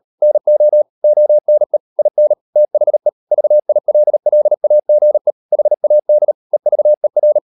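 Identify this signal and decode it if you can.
Morse code, heard as a keyed tone, about 38 wpm, 600 Hz. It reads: NO ONE IN THE VILLAGE HAD EVER
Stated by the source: Morse code practice at 38 WPM, sending Ken Follett's novel Never in English, from CW med Gnister.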